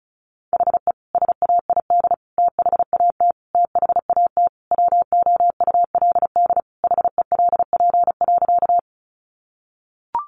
Morse code, heard as a keyed tone, about 34 wpm, 700 Hz. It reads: HE SAID THAT THAT WOULD HELP.
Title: Sentences from Top 200 Words 35wpm